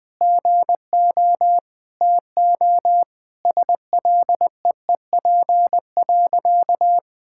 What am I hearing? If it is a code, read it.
Morse code, heard as a keyed tone, about 20 wpm, 700 Hz. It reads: GO TO SLEEP.
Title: Sentences from Top 1000 Words - Review Part 6 - 20wpm